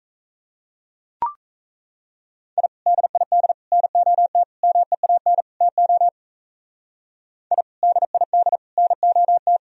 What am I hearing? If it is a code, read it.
Morse code, heard as a keyed tone, about 38 wpm, 700 Hz. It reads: I DID NOT MEAN TO I DID NOT